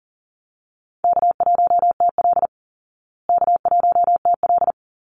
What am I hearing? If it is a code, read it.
Morse code, heard as a keyed tone, about 40 wpm, 700 Hz. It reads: K1TL K1TL
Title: Easy Call Signs Repeated 2x - 40wpm